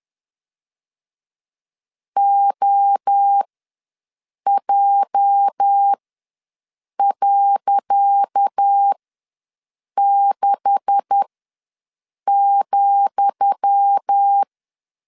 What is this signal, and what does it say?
Morse code, heard as a keyed tone, about 10 wpm, spaced as for 6 wpm, 800 Hz. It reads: OJ.6,
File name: LEZIONI / 10 SIGNS / 10D COMA ws 10wpm 4314